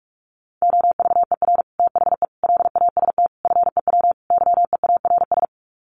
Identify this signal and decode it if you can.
Morse code, heard as a keyed tone, about 44 wpm, 700 Hz. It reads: OVER THE LAST FEW YEARS
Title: Sets of 5 Words Encore 45wpm